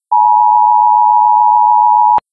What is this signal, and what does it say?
Emergency Alert System (SAME) tones (the 853 and 960 Hz attention tone). It is heard from 0.1 to 2.2 s.